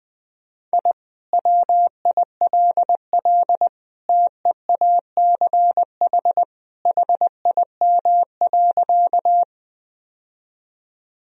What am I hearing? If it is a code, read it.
Morse code, heard as a keyed tone, about 20 wpm, 700 Hz. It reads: I WILL TEACH HIM.
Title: Sentences from Top 1000 Words - Review Part 2 - 20wpm